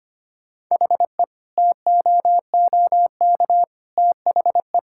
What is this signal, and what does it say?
Morse code, heard as a keyed tone, about 25 wpm, 700 Hz. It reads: HE TOOK THE
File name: practice / 25wpm / Sentences with 3-Word Phrases - 25wpm